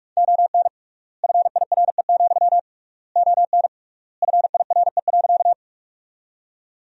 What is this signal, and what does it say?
Morse code, heard as a keyed tone, about 46 wpm, 700 Hz. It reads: ON FIRE, ON FIRE.
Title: 3x Word-Spacing - Sentences from Top 400 Words - 45wpm